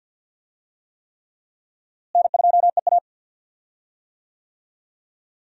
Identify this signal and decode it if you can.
Morse code, heard as a keyed tone, about 48 wpm, 700 Hz. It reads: N2EA
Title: USA Call Signs - 4 Characters - Repeated 3x - 50wpm